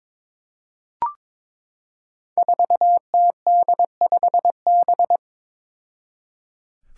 Morse code, heard as a keyed tone, about 22 wpm, 700 Hz. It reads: 4TD5B
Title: Five Challenging Letter-Number Combinations 22wpm